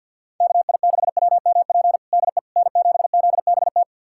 Morse code, heard as a keyed tone, about 51 wpm, 700 Hz. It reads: KI6WMP DE N7ZBT
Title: USA Callsign DE Callsign 50wpm